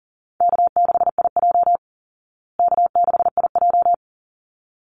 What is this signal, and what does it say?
Morse code, heard as a keyed tone, about 37 wpm, 700 Hz. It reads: K6IJ K6IJ
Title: Easy Call Signs Repeated 2x - 40wpm